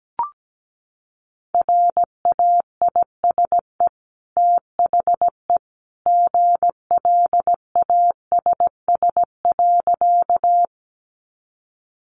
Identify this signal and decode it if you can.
Morse code, heard as a keyed tone, about 17 wpm, 700 Hz. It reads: RAISE THE GLASS.
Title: Sentences from Top 1400 Words 17wpm